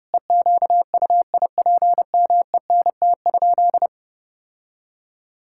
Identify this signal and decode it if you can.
Morse code, heard as a keyed tone, about 31 wpm, 700 Hz. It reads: EQUIPMENT?
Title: Sentences from Top 1000 Words - Review Part 5 - 30wpm